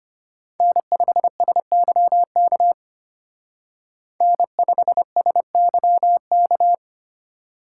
Morse code, heard as a keyed tone, about 33 wpm, 700 Hz. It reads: N5SYK N5SYK